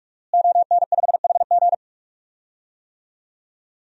Morse code, heard as a keyed tone, about 46 wpm, 700 Hz. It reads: ON5HG
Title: DXCC-100 Call Signs - 5 Characters - Repeated 3x - 45wpm